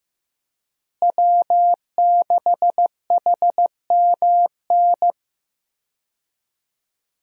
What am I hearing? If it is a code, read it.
Morse code, heard as a keyed tone, about 16 wpm, 700 Hz. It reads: W6HMN
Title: USA Call Signs - 5 Characters - Repeated 3x - 15wpm